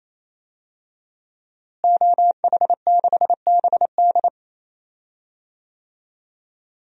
Morse code, heard as a keyed tone, about 28 wpm, 700 Hz. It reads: OH6BD
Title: DXCC-100 Call Signs - 5 Characters - Repeated 3x - 28wpm